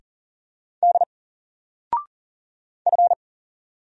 Morse code, heard as a keyed tone, about 39 wpm, 700 Hz. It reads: D F
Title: Single Letter-Number Alt 40wpm